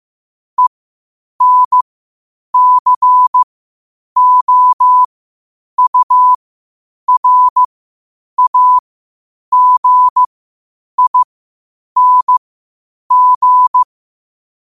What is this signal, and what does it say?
Morse code, heard as a keyed tone, about 15 wpm, spaced as for 8 wpm, 1000 Hz. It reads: ENCOURAGING
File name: qotdfiles / 8wpm